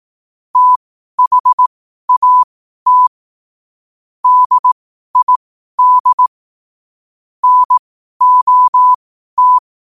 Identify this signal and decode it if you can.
Morse code, heard as a keyed tone, about 18 wpm, spaced as for 12 wpm, 1000 Hz. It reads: THAT DID NOT
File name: qotdfiles / 12wpm